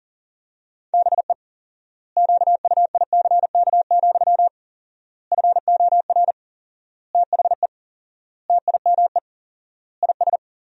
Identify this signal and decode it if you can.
Morse code, heard as a keyed tone, about 37 wpm, 700 Hz. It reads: BE QUICK, FOR THE TIME IS
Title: 4x Word-Spacing - Sentences from Top 500 Words - 40wpm